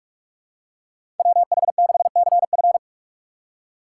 Morse code, heard as a keyed tone, about 44 wpm, 700 Hz. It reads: WH6CF
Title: USA Call Signs - 5 Characters - 6x Character Spacing - 45wpm